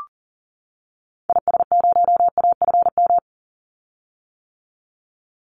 Morse code, heard as a keyed tone, about 40 wpm, 700 Hz. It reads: IS0AFM
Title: Call Signs - 40wpm